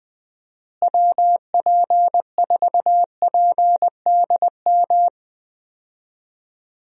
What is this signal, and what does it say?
Morse code, heard as a keyed tone, about 20 wpm, 700 Hz. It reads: WP4PDM
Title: USA Call Signs - 6 Characters - Repeated 3x - 20wpm